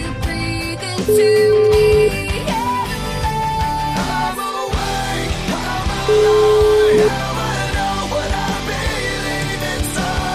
Ringback tone (Europe, 425 Hz), with noise.